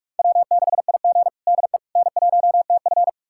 Morse code, heard as a keyed tone, about 41 wpm, 700 Hz. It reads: W6IG DE N1TF
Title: USA Callsign DE Callsign - 4 Characters - 45wpm